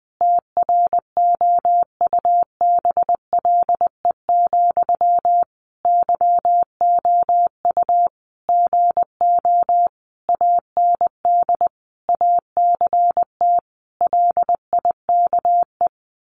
Morse code, heard as a keyed tone, about 20 wpm, 700 Hz. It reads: TROUBLE, YOU GO AND ACT LIKE
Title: Sentences from Top 1200 Words 20wpm